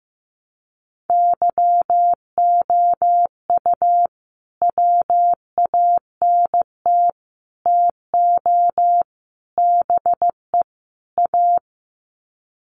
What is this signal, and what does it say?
Morse code, heard as a keyed tone, about 15 wpm, 700 Hz. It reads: YOU WANT TO BE A
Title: Sets of 5 Words Encore2 15wpm